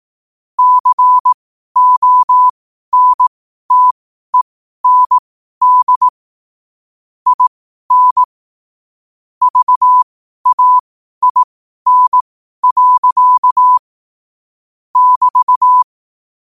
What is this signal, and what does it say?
Morse code, heard as a keyed tone, about 18 wpm, spaced as for 12 wpm, 1000 Hz. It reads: CONTEND IN VAIN. <BT>
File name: qotdfiles / 12wpm